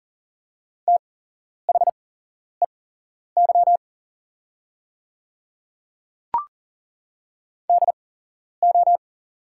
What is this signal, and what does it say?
Morse code, heard as a keyed tone, about 40 wpm, spaced as for 11 wpm, 700 Hz. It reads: THEY DO
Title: 8x Character Spacing - Top 100 Words - 40wpm